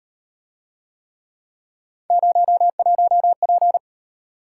Morse code, heard as a keyed tone, about 37 wpm, 700 Hz. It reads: 01P